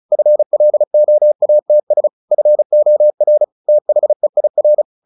Morse code, heard as a keyed tone, about 34 wpm, 600 Hz. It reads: FLOATS FOR THEIR
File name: the old man and the sea / the old man and the sea 1 35wpm